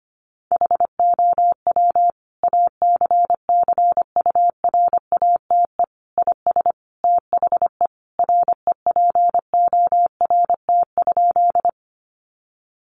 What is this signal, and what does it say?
Morse code, heard as a keyed tone, about 25 wpm, 700 Hz. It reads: HOW ACCURATE IS THE REPORT?
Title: Sentences from Top 1200 Words 25wpm